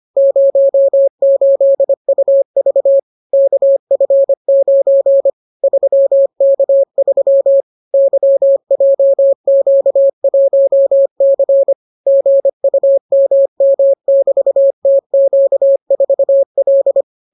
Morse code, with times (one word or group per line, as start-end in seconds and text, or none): 0.2-3.0 08UV
3.3-5.3 KF9
5.6-7.6 3K3
7.9-11.7 YJQ1C
12.1-17.0 GUMM<BT>TQ4L